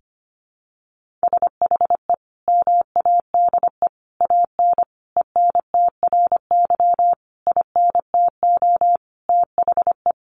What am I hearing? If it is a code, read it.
Morse code, heard as a keyed tone, about 25 wpm, 700 Hz. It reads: SHE MADE AN ENTRY INTO THE
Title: Sentences from Top 1500 Words 25wpm